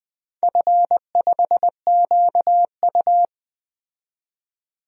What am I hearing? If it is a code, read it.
Morse code, heard as a keyed tone, about 20 wpm, 700 Hz. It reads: F5QU
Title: DXCC-100 Call Signs - 4 Characters - 20wpm